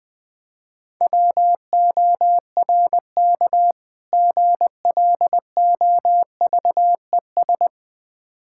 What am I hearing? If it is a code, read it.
Morse code, heard as a keyed tone, about 21 wpm, 690 Hz. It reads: WORK GLOVES
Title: Difficult Words with Hints - 20wpm